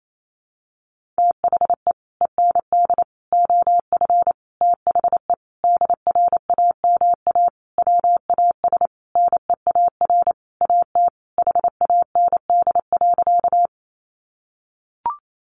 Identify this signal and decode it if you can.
Morse code, heard as a keyed tone, about 28 wpm, 700 Hz. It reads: THE END OF THE DRAMA WAS NEAR AT HAND.
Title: Sentences from Top 1500 Words 28wpm